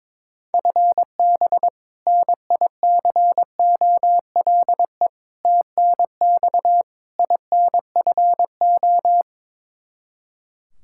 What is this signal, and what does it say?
Morse code, heard as a keyed tone, about 22 wpm, 700 Hz. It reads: FB NICOLE TNX INFO
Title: Practice QSO - Exchange 3 - More Info - 22wpm